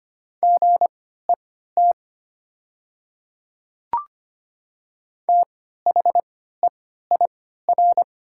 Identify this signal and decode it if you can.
Morse code, heard as a keyed tone, about 25 wpm, spaced as for 14 wpm, 700 Hz. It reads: GET THEIR
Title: 3x Character Spacing - Top 200 Words - 25wpm